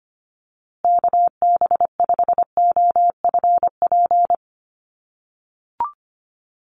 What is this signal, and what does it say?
Morse code, heard as a keyed tone, about 25 wpm, 700 Hz. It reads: KB5OFP
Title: USA Call Signs Repeated 2x 25wpm